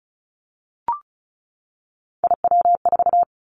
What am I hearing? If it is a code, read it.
Morse code, heard as a keyed tone, about 34 wpm, 700 Hz. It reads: IW4
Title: Callsign Prefixes for Top 25 DXCC - 35wpm